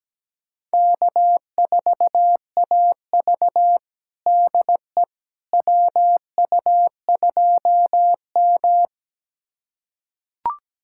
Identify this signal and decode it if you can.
Morse code, heard as a keyed tone, about 17 wpm, 700 Hz. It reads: K4AV DE WU2M
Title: USA Callsign DE Callsign - 4 Characters - 17wpm